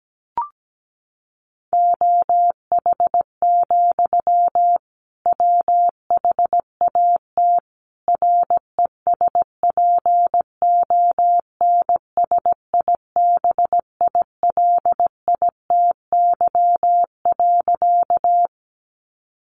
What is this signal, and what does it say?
Morse code, heard as a keyed tone, about 17 wpm, 700 Hz. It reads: OH, WHAT RESPONSIBILITY.
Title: Sentences from Top 1200 Words 17wpm